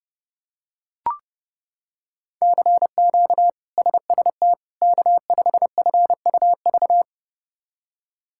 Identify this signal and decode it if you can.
Morse code, heard as a keyed tone, about 30 wpm, 700 Hz. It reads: CQ SST K5FUV
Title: K1USN SST QSO Practice - 30wpm